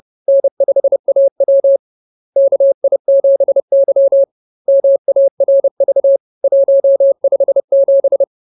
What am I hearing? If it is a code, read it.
Morse code, heard as a keyed tone, about 30 wpm, 550 Hz. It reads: N5AW KI7Y MARV 157
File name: CWT-Chapter-15